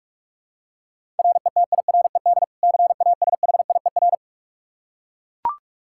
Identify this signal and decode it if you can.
Morse code, heard as a keyed tone, about 42 wpm, 700 Hz. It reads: RETIRED CASHIER
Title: Practice QSO - Exchange 3 - More Info - 45wpm